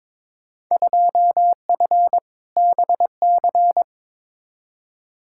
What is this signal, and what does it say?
Morse code, heard as a keyed tone, about 22 wpm, 700 Hz. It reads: 2F BC
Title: Field Day Exchanges Encore - 22wpm